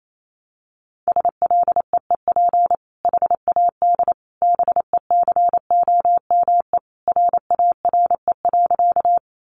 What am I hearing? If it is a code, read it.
Morse code, heard as a keyed tone, about 28 wpm, 700 Hz. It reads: SLEEP HAD BECOME RARE.
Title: Sentences from Top 1100 Words 28wpm